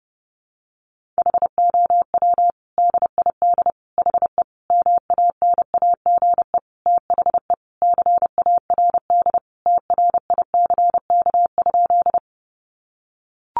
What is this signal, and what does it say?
Morse code, heard as a keyed tone, about 28 wpm, 700 Hz. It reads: HOW DID HE MANAGE THE CARD TRICK?
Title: Sentences from Top 1500 Words 30wpm